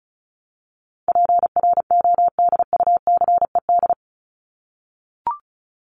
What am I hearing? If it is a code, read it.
Morse code, heard as a keyed tone, about 34 wpm, 700 Hz. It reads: PRODUCED
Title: ED Suffix - Top 100 Words with Base Words 35wpm